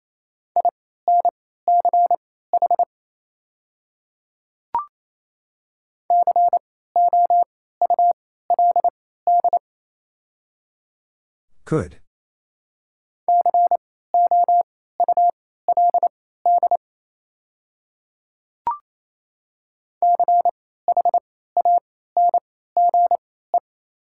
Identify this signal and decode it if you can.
Morse code, heard as a keyed tone, about 27 wpm, spaced as for 16 wpm, 700 Hz. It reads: INCH COULD COULD CHANGE
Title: 3x Character Spacing - Top 500 Words - 28wpm